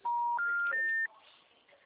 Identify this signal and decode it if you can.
Special information tone.